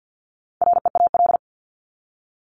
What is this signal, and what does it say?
Morse code, heard as a keyed tone, about 50 wpm, 700 Hz. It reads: REAL
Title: Top 500 Words 50wpm